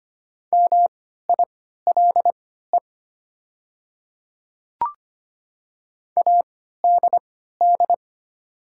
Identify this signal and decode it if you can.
Morse code, heard as a keyed tone, about 25 wpm, spaced as for 14 wpm, 690 Hz. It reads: MILE ADD
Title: 3x Character Spacing - Top 300 Words - 25wpm